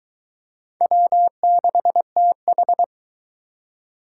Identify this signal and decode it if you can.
Morse code, heard as a keyed tone, about 22 wpm, 700 Hz. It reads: W6TH